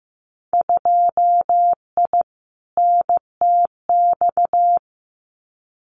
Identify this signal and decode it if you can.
Morse code, heard as a keyed tone, about 15 wpm, 700 Hz. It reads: 2I NTX
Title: Winter Field Day Exchanges Encore 15wpm